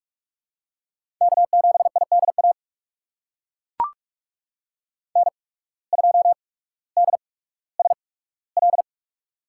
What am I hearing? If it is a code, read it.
Morse code, heard as a keyed tone, about 42 wpm, spaced as for 19 wpm, 700 Hz. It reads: K7IDA N2DSL
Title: USA Call Signs - 5 Characters - 8x Character Spacing - 45wpm